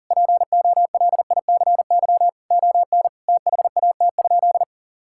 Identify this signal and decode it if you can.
Morse code, heard as a keyed tone, about 40 wpm, 700 Hz. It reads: POLICY ON THAT?
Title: Sentences from Top 1000 Words - Review Part 5 - 40wpm